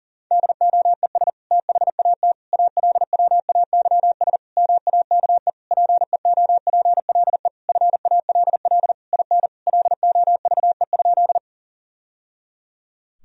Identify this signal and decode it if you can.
Morse code, heard as a keyed tone, about 40 wpm, 700 Hz. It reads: DOES THAT ALWAYS MAKE PEOPLE FALL IN LOVE?
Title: Sentences from Top 1000 Words - Review Part 2 - 40wpm